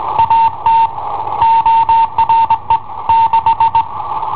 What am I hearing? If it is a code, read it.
Morse code, heard as a keyed tone, about 21 wpm, 910 Hz. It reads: AT ORE 6
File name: i7ohp / cassetta i7ohp B